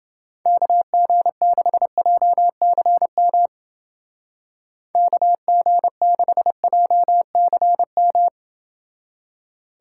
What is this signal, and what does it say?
Morse code, heard as a keyed tone, about 29 wpm, 700 Hz. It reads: KG6JCM KG6JCM